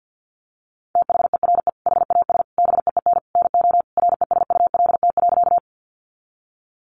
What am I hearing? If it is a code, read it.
Morse code, heard as a keyed tone, about 50 wpm, 700 Hz. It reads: THERE HAS BEEN NO RESULT.
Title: Sentences from Top 700 Words 50wpm